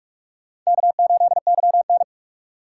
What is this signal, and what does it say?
Morse code, heard as a keyed tone, about 61 wpm, spaced as for 53 wpm, 700 Hz. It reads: K8YN
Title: Speed Racing - USA Call Signs 45wpm